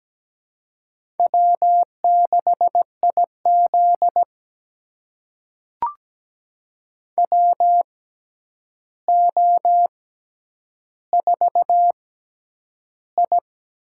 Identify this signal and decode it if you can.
Morse code, heard as a keyed tone, about 17 wpm, spaced as for 9 wpm, 700 Hz. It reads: W6IZ WO4I
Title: USA Call Signs - 4 Characters - 6x Character Spacing - 17wpm